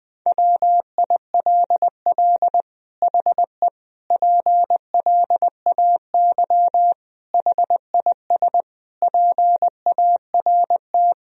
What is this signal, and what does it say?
Morse code, heard as a keyed tone, about 20 wpm, 700 Hz. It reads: WILL HE PLAY HIS PART
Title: Sentences from Top 1000 Words - Review Part 1 - 20wpm